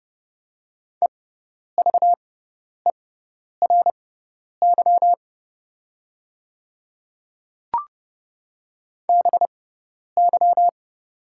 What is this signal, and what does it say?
Morse code, heard as a keyed tone, about 30 wpm, spaced as for 10 wpm, 700 Hz. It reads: EVERY BY
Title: 6x Character Spacing - Top 300 Words - 30wpm